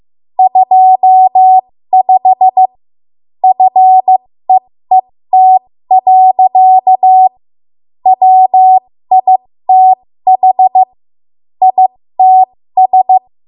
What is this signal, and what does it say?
Morse code, heard as a keyed tone, about 15 wpm, spaced as for 13 wpm, 750 Hz. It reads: 25 FEET. WITH ITS
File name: Morse / 13WPM